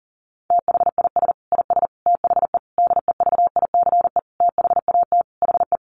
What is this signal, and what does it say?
Morse code, heard as a keyed tone, about 40 wpm, 700 Hz. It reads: THIS IS THE DEVICE THAT HE
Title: Sentences from Top 1500 Words 40wpm